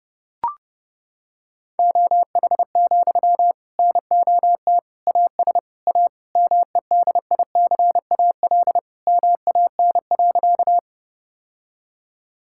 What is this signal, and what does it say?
Morse code, heard as a keyed tone, about 30 wpm, 700 Hz. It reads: OH, NOT AS A MEDICAL MAN.
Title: Sentences from Top 1000 Words - Review Part 5 - 30wpm